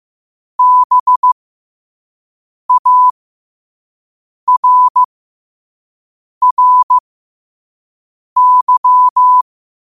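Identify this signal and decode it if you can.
Morse code, heard as a keyed tone, about 15 wpm, spaced as for 5 wpm, 1000 Hz. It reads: BARRY
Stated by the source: Morse quote of the day at 5 wpm.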